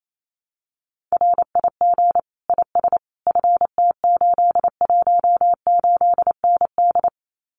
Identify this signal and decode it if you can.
Morse code, heard as a keyed tone, about 28 wpm, 700 Hz. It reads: RIG IS FT818ND
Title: Common QSO Phrases Encore 28wpm